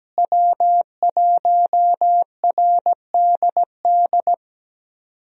Morse code, heard as a keyed tone, about 17 wpm, 700 Hz. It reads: W1RDD
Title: USA Call Signs - 5 Characters - Repeated 3x - 17wpm